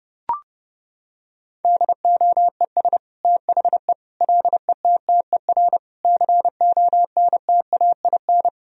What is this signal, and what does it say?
Morse code, heard as a keyed tone, about 29 wpm, 700 Hz. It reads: DOES THE LETTER CONTAIN
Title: Sentences from Top 1000 Words - Review Part 2 - 30wpm